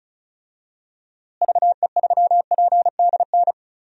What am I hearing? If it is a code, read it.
Morse code, heard as a keyed tone, about 36 wpm, 700 Hz. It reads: VE3PDN
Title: DXCC-100 Call Signs - 6 Characters - Repeated 3x - 35wpm